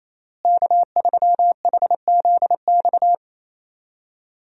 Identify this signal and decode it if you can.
Morse code, heard as a keyed tone, about 28 wpm, 700 Hz. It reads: K3HZX
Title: USA Call Signs - 5 Characters - Repeated 2x - 28wpm